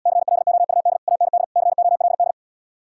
Morse code, heard as a keyed tone, about 77 wpm, 700 Hz. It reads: CLOUD AND COLD